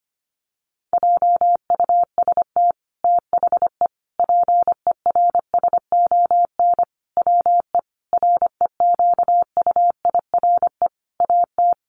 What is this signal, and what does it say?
Morse code, heard as a keyed tone, about 25 wpm, 700 Hz. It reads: JUST THE PERSON WE REQUIRE AT
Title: Sentences from Top 900 Words 25wpm